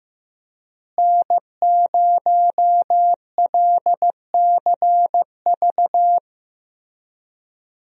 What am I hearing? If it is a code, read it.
Morse code, heard as a keyed tone, about 15 wpm, 700 Hz. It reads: N0LCV
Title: USA Call Signs - 5 Characters - Repeated 3x - 15wpm